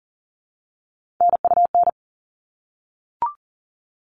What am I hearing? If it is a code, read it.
Morse code, heard as a keyed tone, about 40 wpm, 700 Hz. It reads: NUN
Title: All Cut Numbers - 3-Digit 40wpm